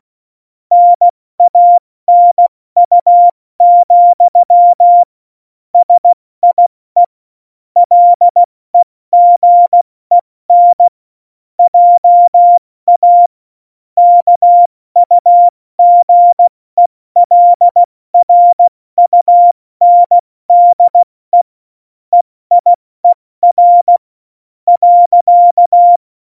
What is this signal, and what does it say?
Morse code, heard as a keyed tone, about 16 wpm, spaced as for 14 wpm, 700 Hz. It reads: NANU, SIE LEGEN JA KUGELRUNDE EIER.
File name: Morsekurs / Witze / Witz 12 Mai, 2020 70 F80